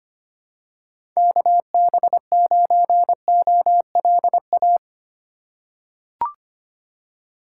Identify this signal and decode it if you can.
Morse code, heard as a keyed tone, about 25 wpm, 700 Hz. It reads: KB9OLA
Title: USA Call Signs - 3x Character Spacing - 25wpm